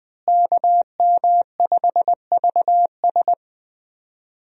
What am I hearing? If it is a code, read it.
Morse code, heard as a keyed tone, about 20 wpm, 700 Hz. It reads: KM5VS
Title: USA Call Signs - 5 Characters - Repeated 3x - 20wpm